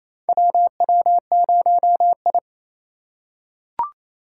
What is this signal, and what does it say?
Morse code, heard as a keyed tone, about 29 wpm, 700 Hz. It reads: WW0I
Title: USA Call Signs - 4 Characters - 3x Character Spacing - 28wpm